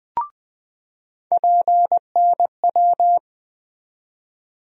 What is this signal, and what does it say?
Morse code, heard as a keyed tone, about 20 wpm, 700 Hz. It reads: PNW
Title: USA Call Sign Postfixes - 3 Letters 20wpm